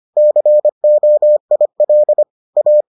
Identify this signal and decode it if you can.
Morse code, heard as a keyed tone, about 25 wpm, 600 Hz. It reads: COIL A